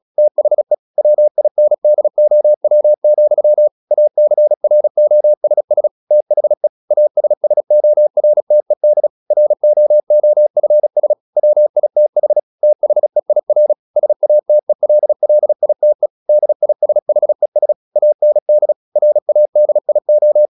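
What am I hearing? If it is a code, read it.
Morse code, heard as a keyed tone, about 36 wpm, 600 Hz. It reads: THE WINDOW, ACROSS THE ASSORTED ROOFS WITH THEIR SATELLITE DISHES AND RADIO
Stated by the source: Morse practice at 35 wpm from CW med Gnister, sending Ken Follett's novel Never in English.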